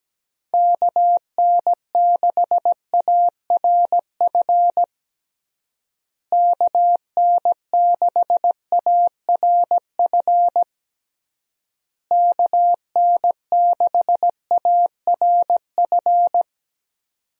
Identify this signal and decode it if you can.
Morse code, heard as a keyed tone, about 17 wpm, 700 Hz. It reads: KN6ARF KN6ARF KN6ARF
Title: USA Call Signs - 6 Characters - Repeated 3x - 17wpm